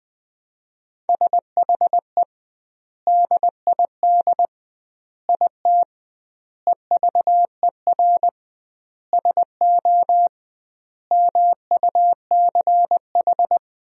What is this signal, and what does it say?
Morse code, heard as a keyed tone, about 20 wpm, 700 Hz. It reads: SHE DID IT EVER SO MUCH
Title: 2x Word-Spacing - Sentences from Top 400 Words - 20wpm